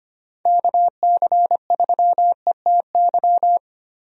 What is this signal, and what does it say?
Morse code, heard as a keyed tone, about 25 wpm, 700 Hz. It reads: KC3ETY
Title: USA Call Signs - 6 Characters - Repeated 2x - 25wpm